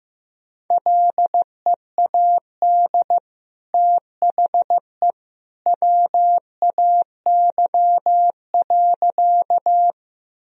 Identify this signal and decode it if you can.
Morse code, heard as a keyed tone, about 15 wpm, 700 Hz. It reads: LEAD THE WAY.
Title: Sentences from Top 1000 Words - Review Part 2 - 15wpm